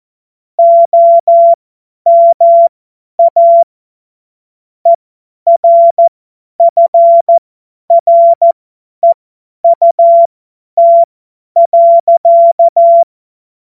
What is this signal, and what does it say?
Morse code, heard as a keyed tone, about 14 wpm, spaced as for 10 wpm, 680 Hz. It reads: OMA ERFREUT.
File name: Morsekurs / Witze / Witz 31. Januar 2023 50 F70